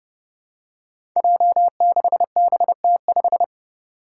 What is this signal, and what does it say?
Morse code, heard as a keyed tone, about 30 wpm, 700 Hz. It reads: J6BT5
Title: Five Challenging Letter-Number Combinations 30wpm